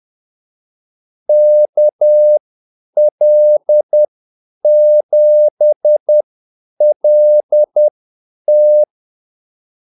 Morse code, heard as a keyed tone, about 10 wpm, spaced as for 8 wpm, 600 Hz. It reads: KL7LT
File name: Callsigns-10-8wpm